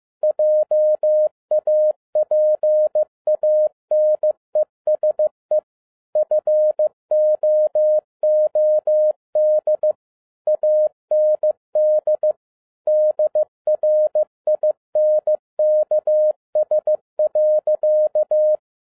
Morse code, heard as a keyed tone, about 15 wpm, 610 Hz. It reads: JAPANESE FOOD AND DRINKS.
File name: 2011-07-mp3 / n47 JOHN K3TN 15wpm